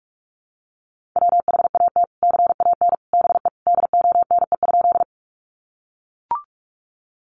Morse code, heard as a keyed tone, about 44 wpm, 700 Hz. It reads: WHAT CAN BE DONE?
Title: Sentences from Top 500 Words 45wpm